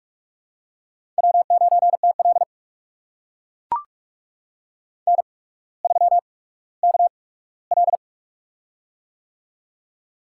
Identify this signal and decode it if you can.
Morse code, heard as a keyed tone, about 44 wpm, spaced as for 19 wpm, 700 Hz. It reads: W9TL N3KL
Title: USA Call Signs - 4 Characters - 8x Character Spacing - 45wpm